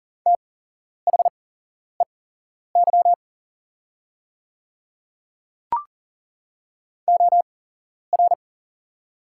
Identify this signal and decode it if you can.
Morse code, heard as a keyed tone, about 40 wpm, spaced as for 11 wpm, 700 Hz. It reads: THEY OR
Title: 8x Character Spacing - Top 200 Words - 40wpm